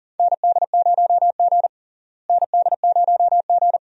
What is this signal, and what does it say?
Morse code, heard as a keyed tone, about 40 wpm, 700 Hz. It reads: ND0G ND0G
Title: USA Call Signs - 4 Characters - Repeated 2x - 40wpm